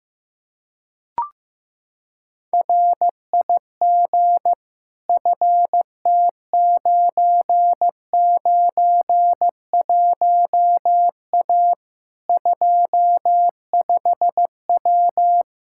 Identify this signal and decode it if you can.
Morse code, heard as a keyed tone, about 15 wpm, 700 Hz. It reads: RIG FT991A 25W
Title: K1USN SST QSO Practice 15wpm